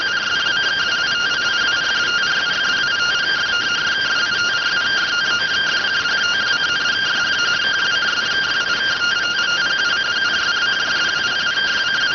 Two-tone teleprinter signal, about 170 Hz shift.